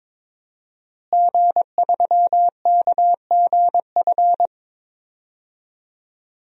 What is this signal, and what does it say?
Morse code, heard as a keyed tone, about 22 wpm, 700 Hz. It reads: G3KGF
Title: DXCC-100 Call Signs - 5 Characters - Repeated 2x - 22wpm